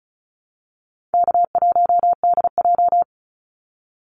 Morse code, heard as a keyed tone, about 34 wpm, 700 Hz. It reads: K1DJ
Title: Easy Call Signs Repeated 2x - 35wpm